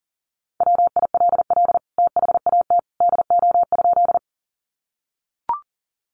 Morse code, heard as a keyed tone, about 40 wpm, 700 Hz. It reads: WILL THAT DO?
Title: Sentences from Top 100 Words 40wpm